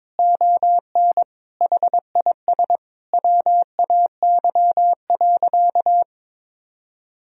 Morse code, heard as a keyed tone, about 22 wpm, 700 Hz. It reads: ON HIS WAY.
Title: Sentences from Top 300 Words and Abbreviations 22wpm